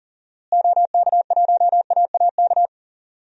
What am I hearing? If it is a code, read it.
Morse code, heard as a keyed tone, about 40 wpm, 700 Hz. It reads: OK1AAK
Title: DXCC-100 Call Signs - 6 Characters - Repeated 2x - 40wpm